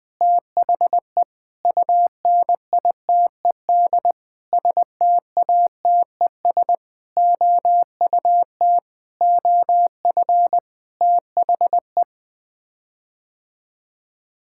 Morse code, heard as a keyed tone, about 20 wpm, 700 Hz. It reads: THE UNITED STATES OUT OF THE
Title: Sets of 8 Words - 20wpm